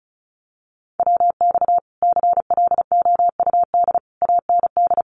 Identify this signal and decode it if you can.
Morse code, heard as a keyed tone, about 34 wpm, 700 Hz. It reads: WX CLOUD AND
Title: Common QSO Phrases Encore 35wpm